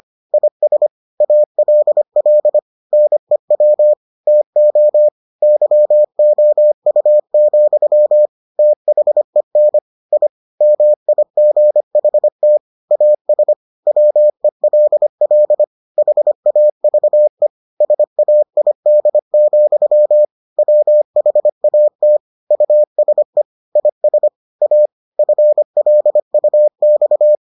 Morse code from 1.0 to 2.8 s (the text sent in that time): ALL